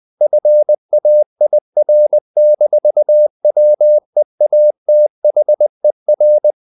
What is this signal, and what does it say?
Morse code, heard as a keyed tone, about 20 wpm, 600 Hz. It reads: FAIR-WEATHER
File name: the old man and the sea part 2 20wpm